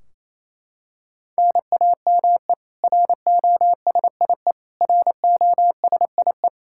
Morse code, heard as a keyed tone, about 28 wpm, 700 Hz. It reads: NAME ROSIE ROSIE
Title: K1USN SST QSO Practice 28wpm